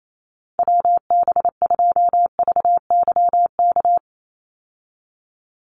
Morse code, heard as a keyed tone, about 28 wpm, 700 Hz. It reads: WB2VYK
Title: Call Signs - 28wpm